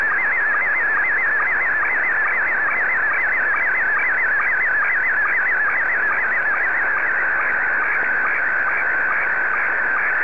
SSTV (slow-scan television), Scottie 1.